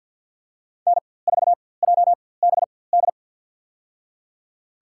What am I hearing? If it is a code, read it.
Morse code, heard as a keyed tone, about 49 wpm, spaced as for 23 wpm, 700 Hz. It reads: N4JBD
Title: USA Call Signs - 5 Characters - 4x Character Spacing - 50wpm